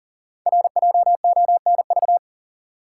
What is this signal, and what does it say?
Morse code, heard as a keyed tone, about 42 wpm, 690 Hz. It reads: RJONV
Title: Code Groups - 5 Letters - 40wpm